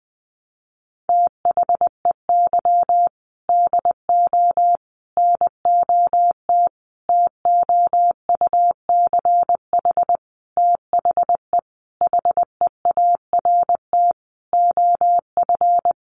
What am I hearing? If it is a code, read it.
Morse code, heard as a keyed tone, about 19 wpm, 700 Hz. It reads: THEY DO NOT TOUCH THE HEART OF
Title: Sentences from Top 1200 Words 20wpm